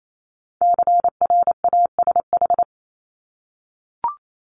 Morse code, heard as a keyed tone, about 28 wpm, 700 Hz. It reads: CRASH
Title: Multi-letter Phonemes 28wpm